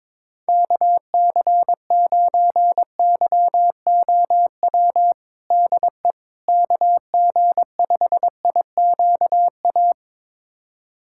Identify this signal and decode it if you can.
Morse code, heard as a keyed tone, about 21 wpm, 700 Hz. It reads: KC9YOW DE KG5IQA
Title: USA Callsign DE Callsign - 6 Characters - 22wpm